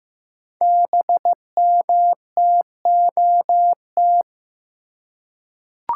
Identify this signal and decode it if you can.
Morse code, heard as a keyed tone, about 15 wpm, 700 Hz. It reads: BMTOT
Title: Code Groups - 5 Letters - 15wpm